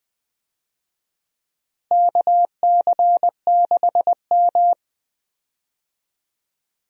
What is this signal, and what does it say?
Morse code, heard as a keyed tone, about 20 wpm, 690 Hz. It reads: KC6M
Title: USA Call Signs - 4 Characters - Repeated 3x - 20wpm